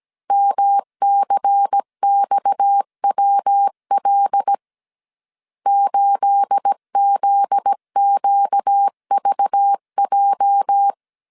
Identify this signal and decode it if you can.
Morse code, heard as a keyed tone, about 17 wpm, 800 Hz. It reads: MCXWL 8ZQVJ
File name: CW / DET / 1 9 1500C S A 16wpm 2828